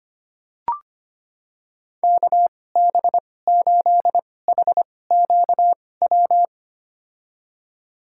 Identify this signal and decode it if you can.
Morse code, heard as a keyed tone, about 25 wpm, spaced as for 18 wpm, 700 Hz. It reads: KB8HQW
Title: USA Call Signs - 6 Characters - 2x Character Spacing - 25wpm